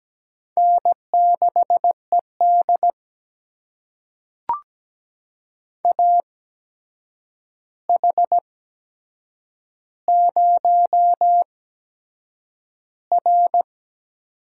Morse code, heard as a keyed tone, about 17 wpm, spaced as for 7 wpm, 700 Hz. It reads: N6ED AH0R
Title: USA Call Signs - 4 Characters - 8x Character Spacing - 17wpm